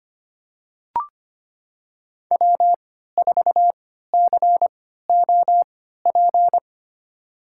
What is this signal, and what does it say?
Morse code, heard as a keyed tone, about 25 wpm, spaced as for 14 wpm, 700 Hz. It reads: W4COP
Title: USA Call Signs - 5 Characters - 3x Character Spacing - 25wpm